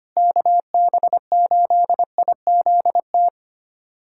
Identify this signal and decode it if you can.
Morse code, heard as a keyed tone, about 25 wpm, 700 Hz. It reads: KB8IZT
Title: USA Callsign DE Callsign 25wpm